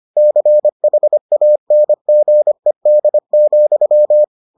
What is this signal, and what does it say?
Morse code, heard as a keyed tone, about 25 wpm, 600 Hz. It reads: CHANGED,